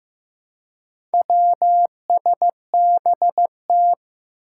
Morse code, heard as a keyed tone, about 15 wpm, 700 Hz. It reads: WSBT
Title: Four Challenging Letter-Number Combinations 15wpm